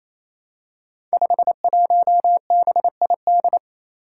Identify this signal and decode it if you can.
Morse code, heard as a keyed tone, about 27 wpm, 700 Hz. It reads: 51BID
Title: Five Challenging Letter-Number Combinations 28wpm